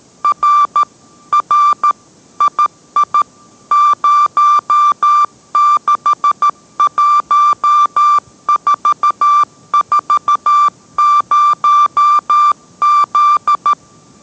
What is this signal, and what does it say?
Morse code, heard as a keyed tone, about 13 wpm, 1180 Hz. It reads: R R II 061440Z